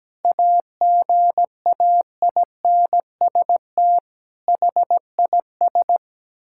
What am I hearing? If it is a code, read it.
Morse code, heard as a keyed tone, about 17 wpm, 700 Hz. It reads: AGAINST HIS
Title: Sentences from Top 1600 Words 17wpm